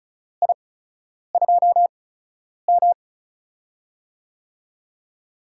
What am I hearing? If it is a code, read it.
Morse code, heard as a keyed tone, about 36 wpm, spaced as for 10 wpm, 700 Hz. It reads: I2M
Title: USA Call Signs - 4 Characters - 8x Character Spacing - 35wpm